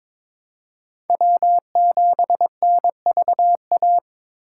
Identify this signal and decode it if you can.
Morse code, heard as a keyed tone, about 21 wpm, 700 Hz. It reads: W7NVA
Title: USA Call Signs - 4x Character Spacing - 22wpm